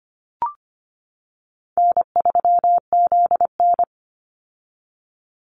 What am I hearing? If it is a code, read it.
Morse code, heard as a keyed tone, about 25 wpm, 700 Hz. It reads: N3ZN
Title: Easy Call Signs - 25wpm